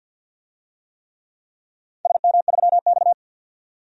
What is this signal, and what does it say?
Morse code, heard as a keyed tone, about 52 wpm, 700 Hz. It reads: SM3X
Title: DXCC-100 Call Signs - 4 Characters - 50wpm